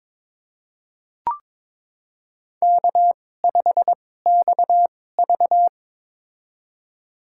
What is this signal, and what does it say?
Morse code, heard as a keyed tone, about 23 wpm, spaced as for 16 wpm, 700 Hz. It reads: K5XV
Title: USA Call Signs - 4 Characters - 2x Character Spacing - 22wpm